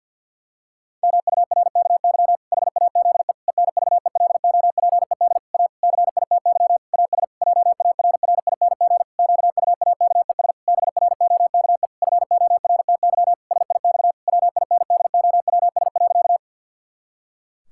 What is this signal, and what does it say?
Morse code, heard as a keyed tone, about 50 wpm, 700 Hz. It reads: MURKY HAZE ENVELOPED A CITY AS JARRING QUAKES BROKE FORTY SIX WINDOWS.
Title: Pangram Challenge v2 - 50wpm